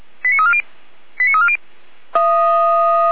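Five-tone selective call (ZVEI), heard as quick runs of five tones, tones of about 70 ms. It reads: 89340 89340
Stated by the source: ZVEI five-tone selective call received on an SDR at 455.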